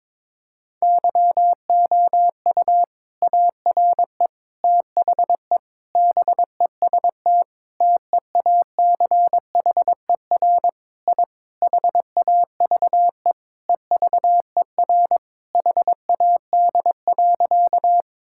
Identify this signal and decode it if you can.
Morse code, heard as a keyed tone, about 22 wpm, 700 Hz. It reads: YOU ARE THE BEST TEACHER I HAVE EVER HAD.
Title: Sentences from Top 1000 Words - Review Part 6 - 22wpm